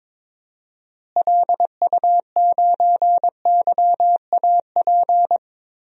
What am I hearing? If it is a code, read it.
Morse code, heard as a keyed tone, about 22 wpm, 700 Hz. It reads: LU9YAP
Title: DXCC-100 Call Signs - 6 Characters - Repeated 3x - 22wpm